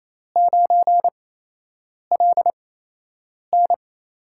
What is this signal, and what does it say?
Morse code, heard as a keyed tone, about 28 wpm, spaced as for 8 wpm, 700 Hz. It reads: 9LN
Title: USA Call Signs - 8x Character Spacing - 28wpm